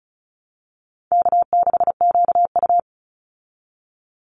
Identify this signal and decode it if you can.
Morse code, heard as a keyed tone, about 34 wpm, 700 Hz. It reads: K6QU
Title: Easy Call Signs Repeated 2x - 35wpm